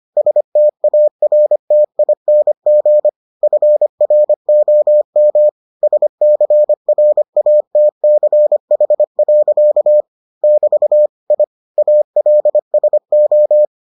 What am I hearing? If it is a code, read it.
Morse code, heard as a keyed tone, about 25 wpm, 600 Hz. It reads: STARTING FROM SCRATCH. <BT> I ALSO